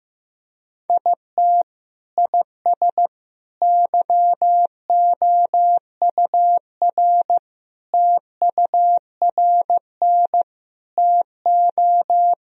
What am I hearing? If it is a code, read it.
Morse code, heard as a keyed tone, about 15 wpm, 700 Hz. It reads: IT IS YOUR TURN TO
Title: Sentences from Top 1000 Words - Review Part 2 - 15wpm